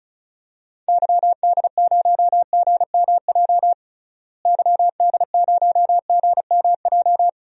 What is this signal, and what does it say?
Morse code, heard as a keyed tone, about 36 wpm, 700 Hz. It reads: YD0GMJ YD0GMJ
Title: DXCC-100 Call Signs - 6 Characters - Repeated 2x - 35wpm